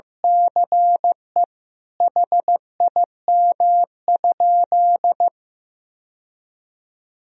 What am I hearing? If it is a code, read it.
Morse code, heard as a keyed tone, about 16 wpm, 700 Hz. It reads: CE HIM?
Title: Sentences from Top 1000 Words - Review Part 3 - 15wpm